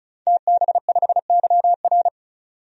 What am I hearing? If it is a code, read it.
Morse code, heard as a keyed tone, about 36 wpm, 700 Hz. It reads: TB5YR
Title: DXCC-100 Call Signs - 5 Characters - 35wpm